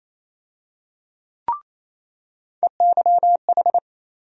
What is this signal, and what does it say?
Morse code, heard as a keyed tone, about 28 wpm, 700 Hz. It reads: EYH
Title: USA Call Sign Postfixes - 3 Letters 28wpm